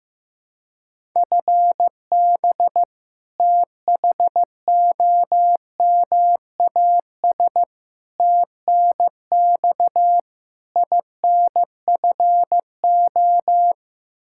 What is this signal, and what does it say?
Morse code, heard as a keyed tone, about 15 wpm, 700 Hz. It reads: FB THOMAS TNX INFO
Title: Practice QSO - Exchange 3 - More Info - 15wpm